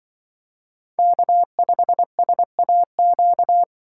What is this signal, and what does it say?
Morse code, heard as a keyed tone, about 25 wpm, 700 Hz. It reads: K5SAQ